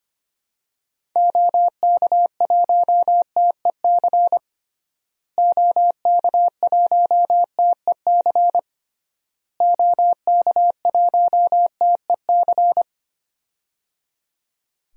Morse code, heard as a keyed tone, about 25 wpm, 700 Hz. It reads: OK1TEC OK1TEC OK1TEC
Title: DXCC-100 Call Signs - 6 Characters - Repeated 3x - 25wpm